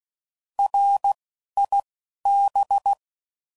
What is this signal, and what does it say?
Morse code, heard as a keyed tone, about 16 wpm, 800 Hz. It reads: R I B
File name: Prefixes / By First Letter / Rs